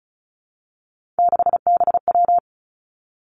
Morse code, heard as a keyed tone, about 36 wpm, 700 Hz. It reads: 6BW
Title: Three Challenging Letter-Number Combinations 35wpm